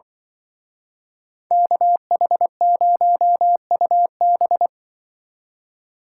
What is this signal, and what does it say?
Morse code, heard as a keyed tone, about 24 wpm, 700 Hz. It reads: KH0UB
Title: Speed Racing - USA 2x2 Call Signs 20wpm